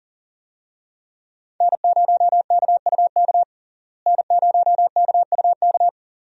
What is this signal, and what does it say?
Morse code, heard as a keyed tone, about 40 wpm, 700 Hz. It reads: N0KUK N0KUK
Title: USA Call Signs - 5 Characters - Repeated 2x - 40wpm